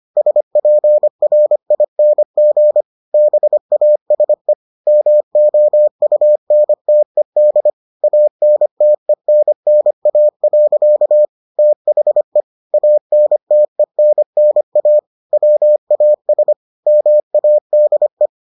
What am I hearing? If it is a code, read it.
Morse code, heard as a keyed tone, about 25 wpm, 600 Hz. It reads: SPRING BASE MOUNTED ANTENNA. THE ANTENNA WAS MADE